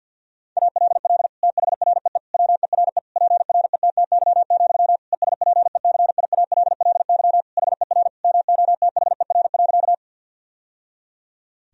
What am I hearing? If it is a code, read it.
Morse code, heard as a keyed tone, about 51 wpm, 700 Hz. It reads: ALL THREE WERE PRETTY, ESPECIALLY HER MOTHER.
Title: Sentences from Top 1000 Words - Review Part 4 - 50wpm